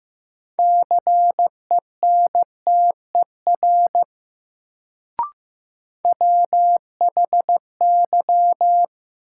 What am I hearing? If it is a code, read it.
Morse code, heard as a keyed tone, about 15 wpm, 700 Hz. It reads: CENTER WHY
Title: Sets of 3 Words - New Word Drill - 15wpm